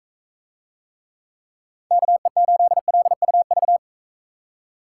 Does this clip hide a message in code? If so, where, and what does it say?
Yes: Morse code from 1.9 to 3.8 s, heard as a keyed tone, about 41 wpm, 700 Hz. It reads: KE8LUV